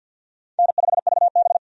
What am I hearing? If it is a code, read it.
Morse code, heard as a keyed tone, about 53 wpm, 700 Hz. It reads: N5VB
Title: USA Call Signs - 4 Characters - 50wpm